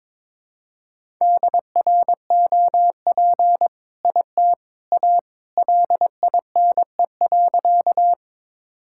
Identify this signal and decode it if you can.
Morse code, heard as a keyed tone, about 22 wpm, 700 Hz. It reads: DROP IT A LINE.
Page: morse-code-ninja-practice-22wpm